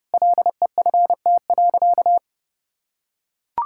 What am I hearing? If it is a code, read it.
Morse code, heard as a keyed tone, about 30 wpm, 700 Hz. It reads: LEFT.
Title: Sentences from Top 1000 Words - Review Part 5 - 30wpm